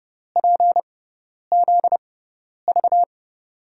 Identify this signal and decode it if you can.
Morse code, heard as a keyed tone, about 30 wpm, spaced as for 10 wpm, 700 Hz. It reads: PZV
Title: USA Call Signs - 5 Characters - 6x Character Spacing - 30wpm